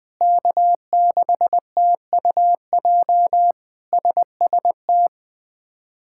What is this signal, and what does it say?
Morse code, heard as a keyed tone, about 20 wpm, 700 Hz. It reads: K6TUJ SST
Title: K1USN SST QSO Practice - 20wpm